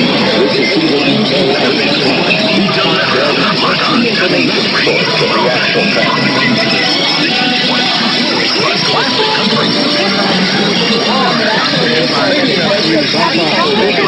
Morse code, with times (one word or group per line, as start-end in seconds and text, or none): none